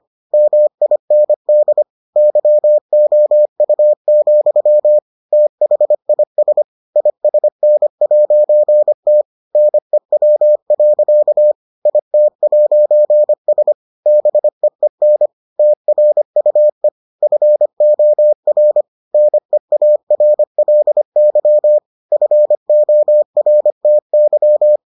Morse code, heard as a keyed tone, about 25 wpm, 600 Hz. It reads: MIND YOU, THIS ISN'T NEW. IT'S BEEN TRUE FOR NEARLY FORTY